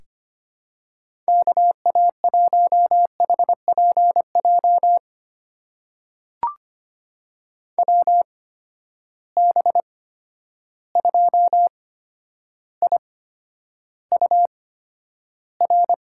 Morse code, heard as a keyed tone, about 25 wpm, spaced as for 11 wpm, 700 Hz. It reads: KA1HPJ WB2IUR
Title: USA Call Signs - 6 Characters - 8x Character Spacing - 25wpm